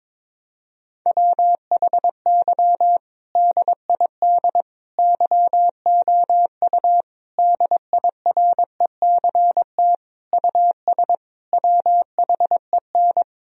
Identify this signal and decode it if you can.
Morse code, heard as a keyed tone, about 22 wpm, 700 Hz. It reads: WHY DID YOU DIRECT US WHEN
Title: Sentences from Top 1000 Words - Review Part 2 - 22wpm